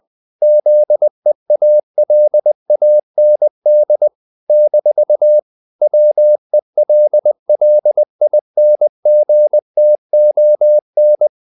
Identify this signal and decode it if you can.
Morse code, heard as a keyed tone, about 20 wpm, 600 Hz. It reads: ZEALAND - WELLINGTON
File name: CW / Practice / NationalCapitals20